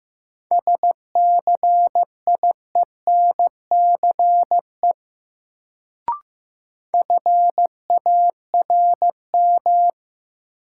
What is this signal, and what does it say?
Morse code, heard as a keyed tone, about 15 wpm, 700 Hz. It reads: SCIENCE FARM
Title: Sentences from Top 1000 Words - Review Part 2 - New Word Drill - 15wpm